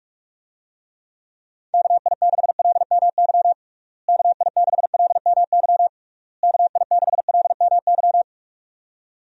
Morse code, heard as a keyed tone, about 42 wpm, 700 Hz. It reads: KI6LMY KI6LMY KI6LMY
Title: USA Call Signs - 6 Characters - Repeated 3x - 45wpm